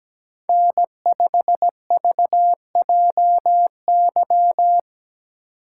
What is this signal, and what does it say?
Morse code, heard as a keyed tone, about 17 wpm, 700 Hz. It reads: N5VJY
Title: USA Call Signs - 6x Character Spacing - 17wpm